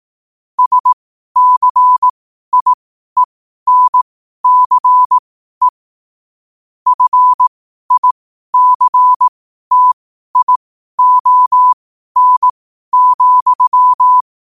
Morse code, heard as a keyed tone, about 18 wpm, spaced as for 12 wpm, 1000 Hz. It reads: SCIENCE FICTION,
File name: qotdfiles / 12wpm